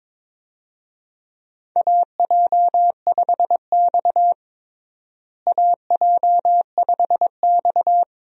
Morse code, heard as a keyed tone, about 22 wpm, 700 Hz. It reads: AJ5X AJ5X
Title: USA Call Signs - 4 Characters - Repeated 2x - 22wpm